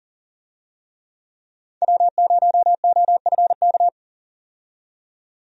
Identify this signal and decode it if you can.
Morse code, heard as a keyed tone, about 39 wpm, 700 Hz. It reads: W0OFK
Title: USA Call Signs - 5 Characters - 40wpm